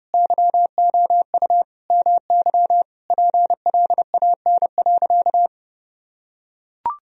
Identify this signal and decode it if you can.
Morse code, heard as a keyed tone, about 32 wpm, 700 Hz. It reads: YOU MY PLAN.
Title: Sentences from Top 1000 Words - Review Part 2 - 30wpm